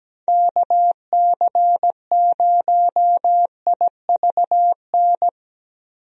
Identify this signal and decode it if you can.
Morse code, heard as a keyed tone, about 17 wpm, 700 Hz. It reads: KC0IVN
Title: USA Call Signs - 6 Characters - Repeated 2x - 17wpm